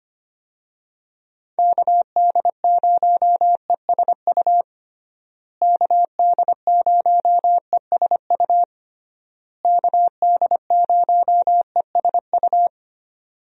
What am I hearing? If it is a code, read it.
Morse code, heard as a keyed tone, about 25 wpm, 700 Hz. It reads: KD0ESU KD0ESU KD0ESU
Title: USA Call Signs - 6 Characters - Repeated 3x - 25wpm